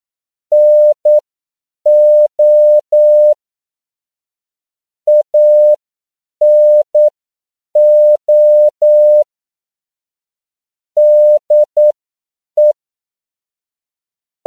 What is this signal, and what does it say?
Morse code, heard as a keyed tone, about 9 wpm, spaced as for 7 wpm, 600 Hz. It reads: NO ANO DE
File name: CW / 10PPM